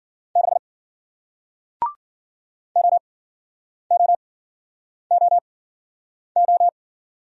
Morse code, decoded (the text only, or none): 6 O O O O